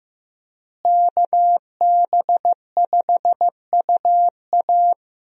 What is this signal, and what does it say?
Morse code, heard as a keyed tone, about 15 wpm, 700 Hz. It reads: KB5UA
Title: USA Call Signs - 5 Characters - Repeated 3x - 15wpm